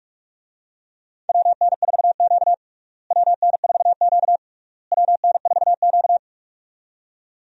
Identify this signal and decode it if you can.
Morse code, heard as a keyed tone, about 46 wpm, 700 Hz. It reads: WN4Q WN4Q WN4Q
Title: USA Call Signs - 4 Characters - Repeated 3x - 45wpm